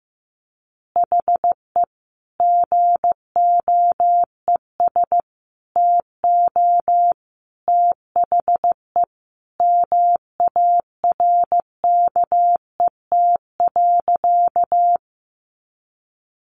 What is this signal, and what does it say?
Morse code, heard as a keyed tone, about 15 wpm, 700 Hz. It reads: HE GOES TO THE MARKET.
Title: Additional Easy Phrases 15wpm